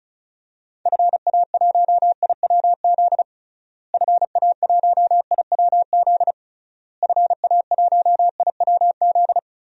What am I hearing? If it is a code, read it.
Morse code, heard as a keyed tone, about 34 wpm, 700 Hz. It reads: FA1IWZ FA1IWZ FA1IWZ
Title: DXCC-100 Call Signs - 6 Characters - Repeated 3x - 35wpm